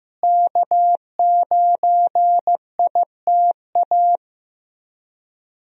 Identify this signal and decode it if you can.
Morse code, heard as a keyed tone, about 15 wpm, 700 Hz. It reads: K9ITA
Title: USA Call Signs - 5 Characters - Repeated 2x - 15wpm